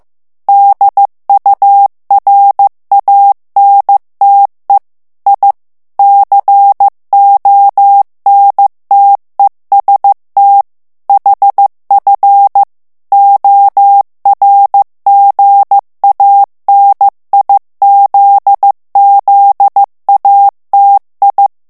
Morse code, heard as a keyed tone, about 15 wpm, 800 Hz. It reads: DURANTE I CONTEST HF ORGANIZZATI
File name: morse mp3 / ITALIEN texte clair 2 vitesse 14 mots minute